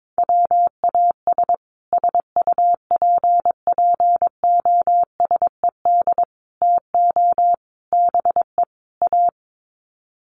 Morse code, heard as a keyed tone, about 22 wpm, 700 Hz. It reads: WAS SUPPOSED TO BE A
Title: Sets of 5 Words Encore 22wpm